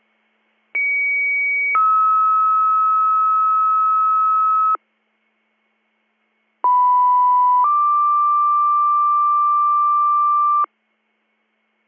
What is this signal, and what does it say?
Two-tone page, heard as a one-second tone, then a three-second tone, about 2250 then 1290 Hz, about 990 then 1180 Hz. It is heard from 0.7 to 10.6 s.